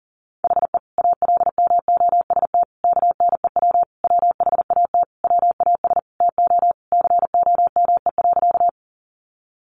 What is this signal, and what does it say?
Morse code, heard as a keyed tone, about 40 wpm, 700 Hz. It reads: HE ALMOST KNEW WHAT WAS TO COME.
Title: Sentences from Top 700 Words 40wpm